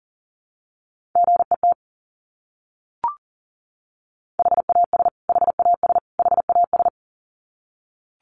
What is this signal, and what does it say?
Morse code, heard as a keyed tone, about 40 wpm, 700 Hz. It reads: GET HAS HAS HAS
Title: Top 200 Words Repeated 40wpm